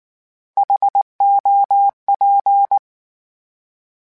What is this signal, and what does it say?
Morse code, heard as a keyed tone, about 19 wpm, 800 Hz. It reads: HOP
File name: H Prefixes / H mp3s